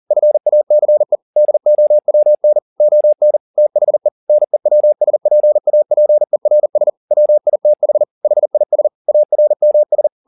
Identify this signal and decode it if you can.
Morse code, heard as a keyed tone, about 40 wpm, 600 Hz. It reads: FACE DOWN ON THE NEWSPAPERS WITH HIS ARMS
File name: the old man and the sea 2 40wpm